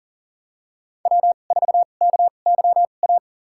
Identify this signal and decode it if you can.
Morse code, heard as a keyed tone, about 40 wpm, spaced as for 29 wpm, 700 Hz. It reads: W4KYA
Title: USA Call Signs - 5 Characters - 2x Character Spacing - 40wpm